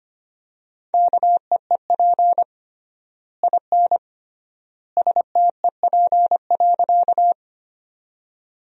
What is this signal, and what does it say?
Morse code, heard as a keyed tone, about 25 wpm, 700 Hz. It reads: KEEP IN STEP.
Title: 3x Word-Spacing - Sentences from Top 400 Words - 25wpm